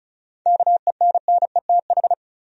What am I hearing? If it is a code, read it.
Morse code, heard as a keyed tone, about 34 wpm, 700 Hz. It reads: KENNETH
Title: USA Common Names - 35wpm